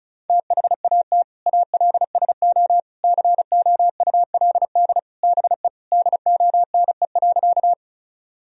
Morse code, heard as a keyed tone, about 34 wpm, 700 Hz. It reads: THAT ALSO COULD BE DONE.
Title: Sentences from Top 1000 Words - Review Part 1 - 35wpm